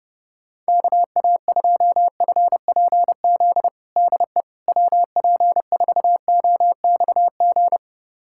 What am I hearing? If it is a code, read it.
Morse code, heard as a keyed tone, about 30 wpm, 700 Hz. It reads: KA2FPZ DE WP4OXG
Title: USA Callsign DE Callsign - 6 Characters - 30wpm